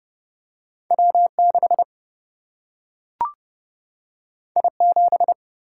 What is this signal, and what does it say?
Morse code, heard as a keyed tone, about 30 wpm, 700 Hz. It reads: W6 I7
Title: Single-Letter Single-Number 30wpm